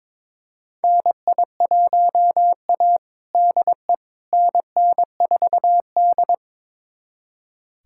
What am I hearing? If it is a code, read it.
Morse code, heard as a keyed tone, about 22 wpm, 700 Hz. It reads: NI1A DE NN4D
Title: USA Callsign DE Callsign - 4 Characters - 22wpm